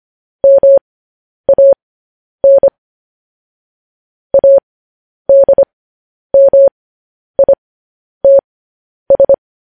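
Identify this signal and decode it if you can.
Morse code, heard as a keyed tone, about 25 wpm, spaced as for 10 wpm, 550 Hz. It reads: MAN ADMITS
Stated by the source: Morse code news headlines broadcast.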